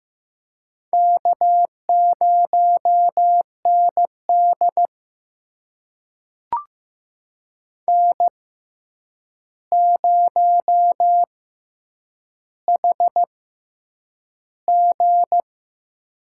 Morse code, heard as a keyed tone, about 15 wpm, spaced as for 8 wpm, 700 Hz. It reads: K0ND N0HG
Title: USA Call Signs - 4 Characters - 6x Character Spacing - 15wpm